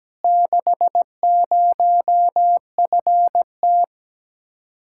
Morse code, heard as a keyed tone, about 17 wpm, 700 Hz. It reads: 60FT
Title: Speed-Racing Common QSO Phrases - 17wpm